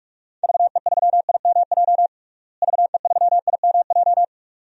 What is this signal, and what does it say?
Morse code, heard as a keyed tone, about 46 wpm, 700 Hz. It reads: VE3IMJ VE3IMJ
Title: DXCC-100 Call Signs - 6 Characters - Repeated 2x - 45wpm